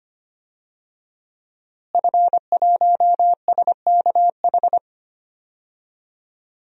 Morse code, heard as a keyed tone, about 25 wpm, 700 Hz. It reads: F1SKH
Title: DXCC-100 Call Signs - 5 Characters - Repeated 3x - 25wpm